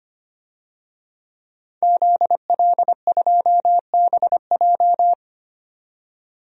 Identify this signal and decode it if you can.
Morse code, heard as a keyed tone, about 25 wpm, 700 Hz. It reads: ZL2BJ
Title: DXCC-100 Call Signs - 5 Characters - 25wpm